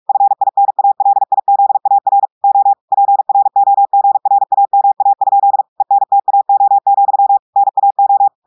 Morse code, heard as a keyed tone, about 46 wpm, 800 Hz. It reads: FINALIZAR O PROGRAMA? ENTAO, NAO
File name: audios / portugais / portugais texte en clair 2 vitesse 45 mots minutes